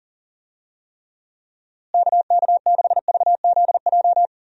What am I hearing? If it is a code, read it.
Morse code, heard as a keyed tone, about 40 wpm, 700 Hz. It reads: KK6VZJ
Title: USA Call Signs - 6 Characters - Repeated 2x - 40wpm